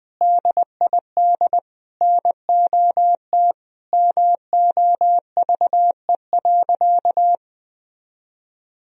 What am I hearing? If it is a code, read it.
Morse code, heard as a keyed tone, about 21 wpm, 700 Hz. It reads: DID NOT MOVE.
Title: Sentences from Top 1000 Words - Review Part 1 - 20wpm